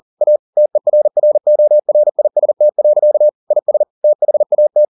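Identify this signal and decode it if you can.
Morse code, heard as a keyed tone, about 40 wpm, 600 Hz. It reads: A TERRORIST. IS THAT
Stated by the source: Morse code at 40 wpm, practice text, CW med Gnister.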